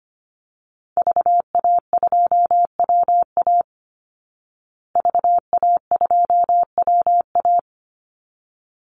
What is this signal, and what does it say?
Morse code, heard as a keyed tone, about 25 wpm, 700 Hz. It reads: VA2WA VA2WA
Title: Call Signs Repeated 3x - 25wpm